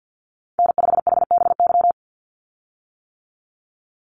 Morse code, heard as a keyed tone, about 50 wpm, 700 Hz. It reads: N5HBY
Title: USA Call Signs 50wpm